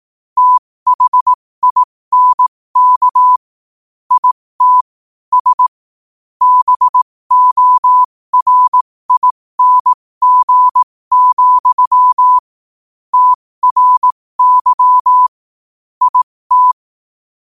Morse code, heard as a keyed tone, about 18 wpm, spaced as for 15 wpm, 1000 Hz. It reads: THINK IT S BORING, TRY IT
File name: qotdfiles / 15wpm